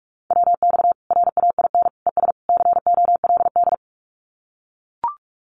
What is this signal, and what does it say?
Morse code, heard as a keyed tone, about 44 wpm, 700 Hz. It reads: WX RAIN ES COLD
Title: Common QSO Phrases 45wpm